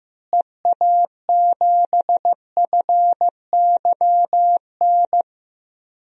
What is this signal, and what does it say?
Morse code, heard as a keyed tone, about 15 wpm, 700 Hz. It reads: EA7FYN
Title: DXCC-100 Call Signs - 6 Characters - Repeated 2x - 15wpm